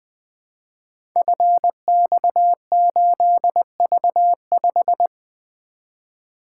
Morse code, heard as a keyed tone, about 20 wpm, 700 Hz. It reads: FX8V5
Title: Code Groups - 5 Letter:Numbers - 20wpm